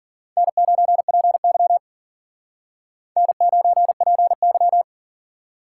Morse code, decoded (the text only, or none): N9PY N9PY